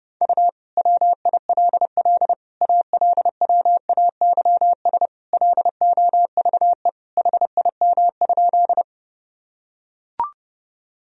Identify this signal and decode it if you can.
Morse code, heard as a keyed tone, about 30 wpm, 700 Hz. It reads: U WILL ALWAYS LOVE HIM?
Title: Sentences from Top 400 Words and Abbreviations 30wpm